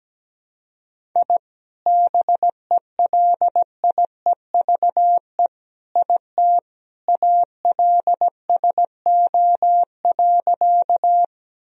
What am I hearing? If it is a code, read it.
Morse code, heard as a keyed tone, about 17 wpm, 700 Hz. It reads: I BELIEVE IT ALSO.
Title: Sentences from Top 1000 Words - Review Part 1 - 17wpm